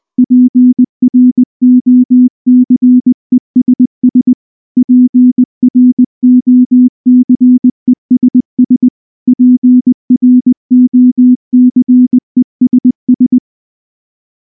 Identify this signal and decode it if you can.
Morse code, heard as a keyed tone, about 19 wpm, 250 Hz. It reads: PROCESS PROCESS PROCESS
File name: morse / morse-training-20WPM-3